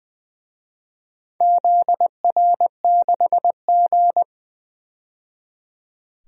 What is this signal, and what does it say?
Morse code, heard as a keyed tone, about 20 wpm, 700 Hz. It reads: ZR6G
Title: DXCC-100 Call Signs - 4 Characters - 20wpm